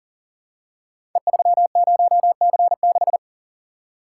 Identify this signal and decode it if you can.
Morse code, heard as a keyed tone, about 39 wpm, 700 Hz. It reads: E30C6
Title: Code Groups - 5 Letter:Numbers - 40wpm